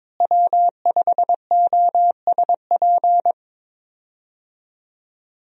Morse code, heard as a keyed tone, about 22 wpm, 700 Hz. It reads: W5OSP
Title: USA Call Signs - 5 Characters - Repeated 3x - 22wpm